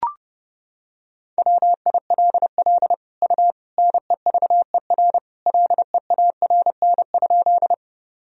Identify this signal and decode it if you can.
Morse code, heard as a keyed tone, about 32 wpm, 700 Hz. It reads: WILL U NEVER LEARN?
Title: Sentences from Top 300 Words and Abbreviations 30wpm